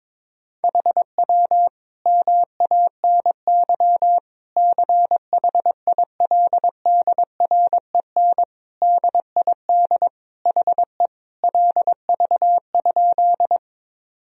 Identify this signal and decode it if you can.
Morse code, heard as a keyed tone, about 21 wpm, 700 Hz. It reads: HW MANY CHILDREN DID HE LV?
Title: Sentences from Top 400 Words and Abbreviations 22wpm